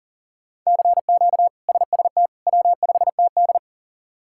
Morse code, heard as a keyed tone, about 40 wpm, 700 Hz. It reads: CQ SST W5TD
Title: K1USN SST QSO Practice - 40wpm